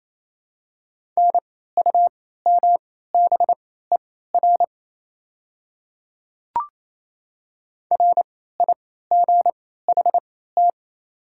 Morse code, heard as a keyed tone, about 28 wpm, spaced as for 16 wpm, 700 Hz. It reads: NUMBER RIGHT
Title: 3x Character Spacing - Top 500 Words - 28wpm